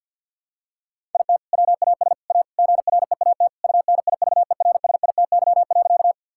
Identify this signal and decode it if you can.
Morse code, heard as a keyed tone, about 50 wpm, 700 Hz. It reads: IT WAS A GREAT UNIVERSITY.
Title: Sentences from Top 1000 Words - Review Part 5 - 50wpm